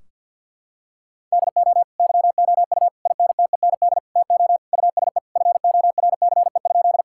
Morse code, heard as a keyed tone, about 50 wpm, 700 Hz. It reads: DO YOU INTEND TO USE FORCE?
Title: Sentences from Top 1600 Words 50wpm